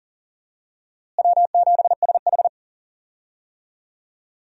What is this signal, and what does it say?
Morse code, heard as a keyed tone, about 40 wpm, 700 Hz. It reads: W7SH
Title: USA Call Signs - 4 Characters - Repeated 3x - 40wpm